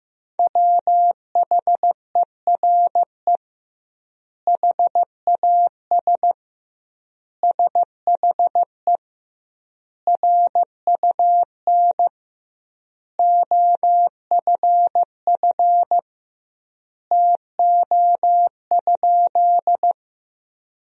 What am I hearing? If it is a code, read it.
Morse code, heard as a keyed tone, about 15 wpm, 700 Hz. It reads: WHERE HAS SHE RUN OFF TO?
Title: 2x Word-Spacing - Sentences from Top 300 Words - 15wpm